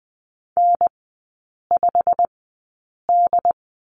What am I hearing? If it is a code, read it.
Morse code, heard as a keyed tone, about 19 wpm, spaced as for 8 wpm, 700 Hz. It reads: N5D
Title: 138 Lesson 23e - Letter f - Three Characters 20wpm